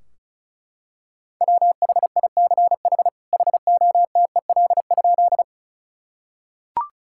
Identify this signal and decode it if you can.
Morse code, heard as a keyed tone, about 36 wpm, 700 Hz. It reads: WHICH HOTEL?
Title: Sentences from Top 1600 Words 35wpm